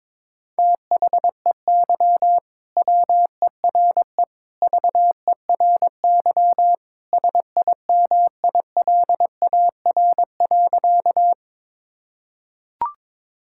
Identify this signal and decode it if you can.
Morse code, heard as a keyed tone, about 22 wpm, 700 Hz. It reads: THEY WERE VERY SIMILAR.
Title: Sentences from Top 1000 Words - Review Part 5 - 22wpm